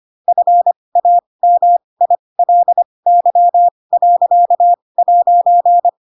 Morse code, heard as a keyed tone, about 25 wpm, spaced as for 20 wpm, 700 Hz. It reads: FAMILY.'
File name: Pinocchio / Pinocchio ch7 15 25 ik1hge